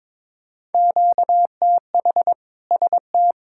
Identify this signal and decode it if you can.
Morse code, heard as a keyed tone, about 22 wpm, 700 Hz. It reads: QTH ST
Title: Practice QSO - Exchange 1 - Beginning - 22wpm